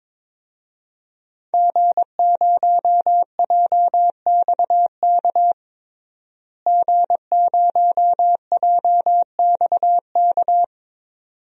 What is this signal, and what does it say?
Morse code, heard as a keyed tone, about 22 wpm, 700 Hz. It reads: G0JXK G0JXK
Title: DXCC-100 Call Signs - 5 Characters - Repeated 2x - 22wpm